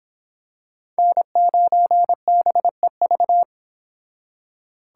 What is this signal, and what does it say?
Morse code, heard as a keyed tone, about 26 wpm, 700 Hz. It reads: N9BEV